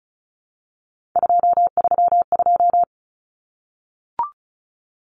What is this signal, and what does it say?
Morse code, heard as a keyed tone, about 34 wpm, 700 Hz. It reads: 232
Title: Numbers - 3-Digit 35wpm